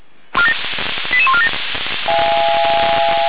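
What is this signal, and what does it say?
Five-tone selective call (ZVEI), heard as quick runs of five tones, tones of about 70 ms. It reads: 88147
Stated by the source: ZVEI five-tone selective call received on an SDR.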